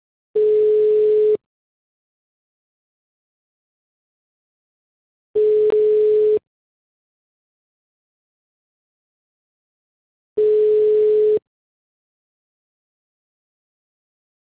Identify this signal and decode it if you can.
Ringback tone (Europe, 425 Hz).
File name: out- -2015- - - .21130